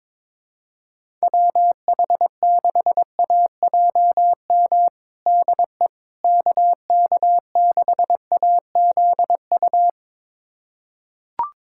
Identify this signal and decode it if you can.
Morse code, heard as a keyed tone, about 22 wpm, 700 Hz. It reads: WH6AJM DE KK6AZU
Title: USA Callsign DE Callsign 22wpm